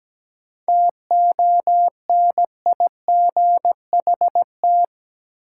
Morse code, heard as a keyed tone, about 17 wpm, 700 Hz. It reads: TONIGHT
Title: Sentences from Top 600 Words - New Word Drill - 17wpm